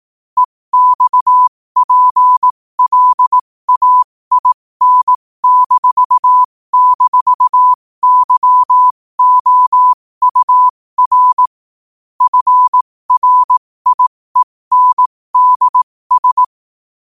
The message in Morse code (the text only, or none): EXPLAIN--YOUR FRIENDS